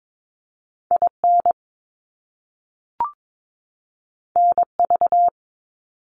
Morse code, heard as a keyed tone, about 22 wpm, 700 Hz. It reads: IN NV
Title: US States 22wpm